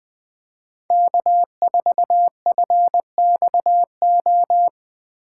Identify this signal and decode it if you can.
Morse code, heard as a keyed tone, about 23 wpm, 700 Hz. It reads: K4FXO
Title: Speed Racing - USA Call Signs 20wpm